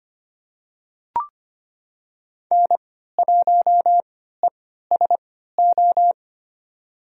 Morse code, heard as a keyed tone, about 25 wpm, spaced as for 14 wpm, 700 Hz. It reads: N1ESO
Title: USA Call Signs - 5 Characters - 3x Character Spacing - 25wpm